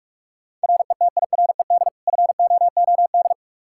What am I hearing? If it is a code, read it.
Morse code, heard as a keyed tone, about 46 wpm, 700 Hz. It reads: RETIRED FOOD
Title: Practice QSO - Exchange 3 - More Info - 45wpm